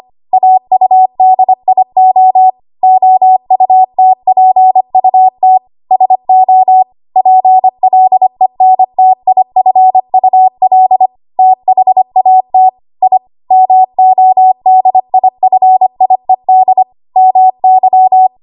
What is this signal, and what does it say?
Morse code, heard as a keyed tone, about 25 wpm, 750 Hz. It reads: AUDIO OUTPUT SO PLENTIFUL THAT I MODIFIED MY
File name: Archive / 25 WPM / 25WPM